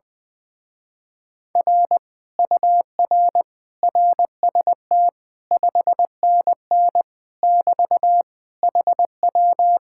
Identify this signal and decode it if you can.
Morse code, heard as a keyed tone, about 20 wpm, 700 Hz. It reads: R UR RST 5NN <BT> HW
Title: Speed-Racing Common QSO Phrases - 20wpm